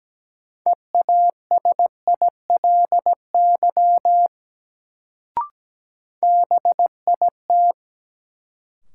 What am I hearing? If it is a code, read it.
Morse code, heard as a keyed tone, about 17 wpm, 700 Hz. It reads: EASILY BIT
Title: Sentences from Top 900 Words - New Word Drill - 17wpm